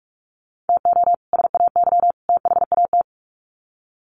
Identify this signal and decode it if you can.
Morse code, heard as a keyed tone, about 44 wpm, 700 Hz. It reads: TO SAY THAT
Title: Sets of 3 Words 45wpm